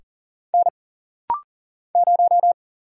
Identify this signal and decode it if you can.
Morse code, heard as a keyed tone, about 40 wpm, 700 Hz. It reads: N 0
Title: Single Letter-Number - Rapid-Fire 40wpm